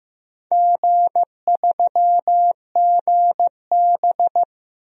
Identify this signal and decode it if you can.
Morse code, heard as a keyed tone, about 15 wpm, 700 Hz. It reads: G3GB
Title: DXCC-100 Call Signs - 4 Characters - 15wpm